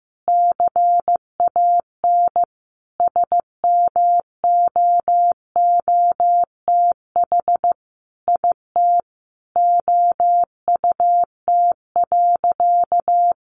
Morse code, heard as a keyed tone, about 15 wpm, 700 Hz. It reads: CAN SMOOTH IT OUT.
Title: Sentences from Top 1400 Words 15wpm